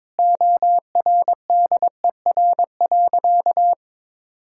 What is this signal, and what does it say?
Morse code, heard as a keyed tone, about 22 wpm, 700 Hz. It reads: ORDER.
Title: Sentences from Top 1600 Words 22wpm